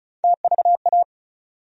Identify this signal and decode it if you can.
Morse code, heard as a keyed tone, about 34 wpm, 700 Hz. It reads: TVA
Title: USA Call Sign Postfixes - 3 Letters 35wpm